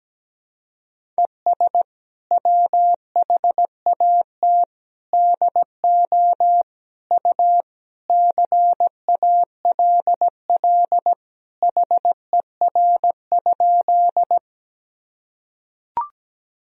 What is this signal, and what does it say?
Morse code, heard as a keyed tone, about 17 wpm, 700 Hz. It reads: ES WHAT DO U CALL HER?
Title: Sentences from Top 100 Words and Abbreviations 17wpm